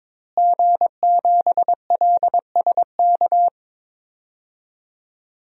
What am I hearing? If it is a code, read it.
Morse code, heard as a keyed tone, about 22 wpm, 700 Hz. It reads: G7LSK
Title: DXCC-100 Call Signs - 5 Characters - Repeated 2x - 22wpm